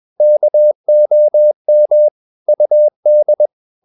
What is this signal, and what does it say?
Morse code, heard as a keyed tone, about 21 wpm, 600 Hz. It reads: KOM UD
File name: Ferie QRS 21wpm